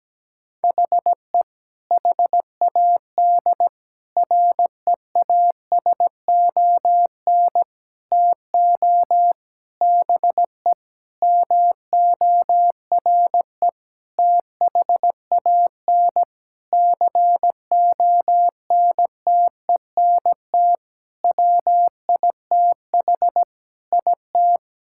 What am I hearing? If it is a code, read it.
Morse code, heard as a keyed tone, about 17 wpm, 700 Hz. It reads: HE HAD REASON TO BE MORE THAN CONTENT WITH IT
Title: Sentences from Top 1000 Words - Review Part 6 - 17wpm